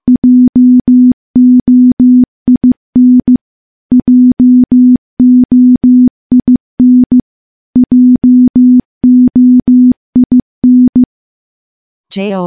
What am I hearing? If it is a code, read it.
Morse code, heard as a keyed tone, about 15 wpm, 250 Hz. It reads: JOIN JOIN JOIN
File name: morse / morse-training-15WPM